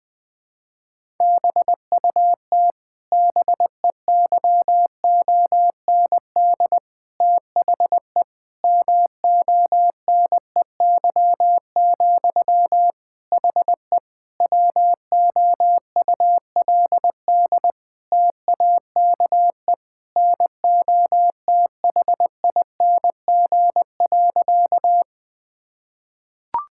Morse code, heard as a keyed tone, about 20 wpm, 700 Hz. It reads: BUT BEYOND THE MONEY, HE WOULD TAKE NOTHING.
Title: Sentences from Top 1000 Words - Review Part 6 - 20wpm